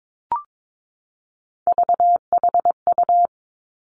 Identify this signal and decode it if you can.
Morse code, heard as a keyed tone, about 22 wpm, 700 Hz. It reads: VHU
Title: Three Challenging Letter-Number Combinations 22wpm